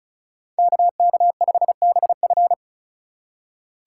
Morse code, heard as a keyed tone, about 36 wpm, 690 Hz. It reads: KK5BF
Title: USA Call Signs - 5 Characters - 3x Character Spacing - 35wpm